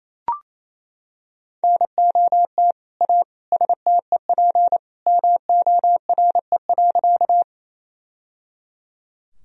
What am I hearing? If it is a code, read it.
Morse code, heard as a keyed tone, about 28 wpm, 700 Hz. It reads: NOT A STEP MORE.
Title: Sentences from Top 1000 Words - Review Part 2 - 28wpm